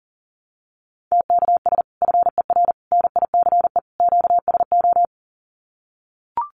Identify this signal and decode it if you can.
Morse code, heard as a keyed tone, about 40 wpm, 700 Hz. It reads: TKS FER NICE QSO
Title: Common QSO Phrases 40wpm